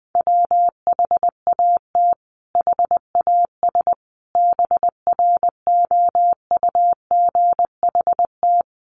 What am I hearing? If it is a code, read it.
Morse code, heard as a keyed tone, about 20 wpm, 700 Hz. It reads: WHAT HAS BROUGHT
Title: Sentences from Top 1500 Words 20wpm